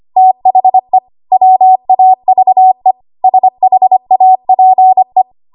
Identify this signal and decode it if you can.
Morse code, heard as a keyed tone, about 25 wpm, 750 Hz. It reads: THE WAVE SHAPE
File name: Archive / 25 WPM / 25WPM